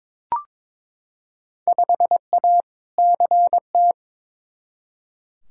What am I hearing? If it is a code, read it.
Morse code, heard as a keyed tone, about 22 wpm, 700 Hz. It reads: 5A CT
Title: Field Day Exchanges - 22wpm